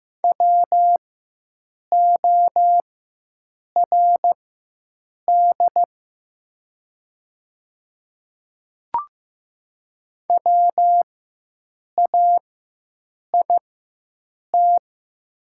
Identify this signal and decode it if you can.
Morse code, heard as a keyed tone, about 15 wpm, spaced as for 7 wpm, 700 Hz. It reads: WORD WAIT
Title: 4x Character Spacing - Top 500 Words - 15wpm